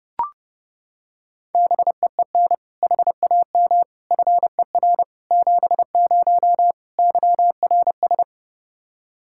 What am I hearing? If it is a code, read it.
Morse code, heard as a keyed tone, about 32 wpm, 700 Hz. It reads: BEEN HAM FER 70 YRS
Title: Practice QSO - Exchange 3 - More Info - 30wpm